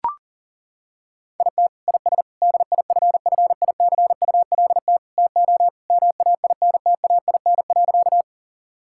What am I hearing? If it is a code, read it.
Morse code, heard as a keyed tone, about 40 wpm, 700 Hz. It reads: IT IS DIFFICULT TO MAINTAIN.
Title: Sentences from Top 1000 Words - Review Part 5 - 40wpm